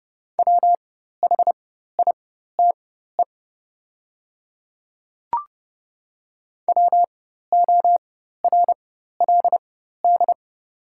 Morse code, heard as a keyed tone, about 30 wpm, spaced as for 14 wpm, 700 Hz. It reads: WHITE WORLD
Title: 4x Character Spacing - Top 500 Words - 30wpm